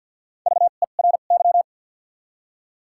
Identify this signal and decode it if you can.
Morse code, heard as a keyed tone, about 53 wpm, spaced as for 37 wpm, 700 Hz. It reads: VERY
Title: 2x Character Spacing - Top 300 Words - 50wpm